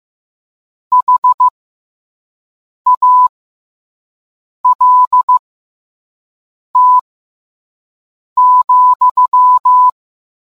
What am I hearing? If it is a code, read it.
Morse code, heard as a keyed tone, about 15 wpm, spaced as for 5 wpm, 1000 Hz. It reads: HALT,